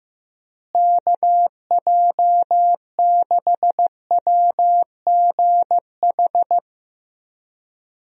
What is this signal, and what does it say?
Morse code, heard as a keyed tone, about 15 wpm, 700 Hz. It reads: KJ6WGH
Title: USA Call Signs - 6 Characters - Repeated 2x - 15wpm